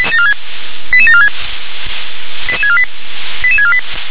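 Five-tone selective call (ZVEI), heard as quick runs of five tones, tones of about 70 ms. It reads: 88746 88648 88648
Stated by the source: ZVEI five-tone selective call received on an SDR.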